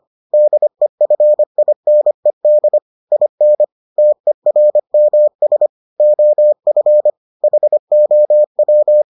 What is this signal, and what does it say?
Morse code, heard as a keyed tone, about 25 wpm, 600 Hz. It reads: DEFINED IN TERMS OF HOW